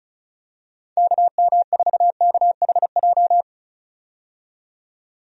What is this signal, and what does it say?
Morse code, heard as a keyed tone, about 36 wpm, 700 Hz. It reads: KM4KHJ
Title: USA Call Signs - 6 Characters - 35wpm